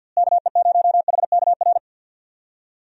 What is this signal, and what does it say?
Morse code, heard as a keyed tone, about 49 wpm, 700 Hz. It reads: KE0HKR
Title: USA Call Signs - 6 Characters - Repeated 3x - 50wpm